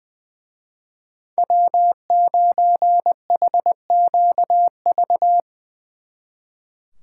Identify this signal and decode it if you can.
Morse code, heard as a keyed tone, about 23 wpm, 700 Hz. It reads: W9HQV
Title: Speed Racing - USA Call Signs 20wpm